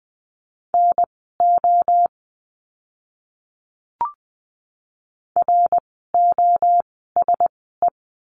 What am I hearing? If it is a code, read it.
Morse code, heard as a keyed tone, about 20 wpm, spaced as for 14 wpm, 700 Hz. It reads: NO ROSE
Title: 054 Lesson 10f - Letter r - Words 20wpm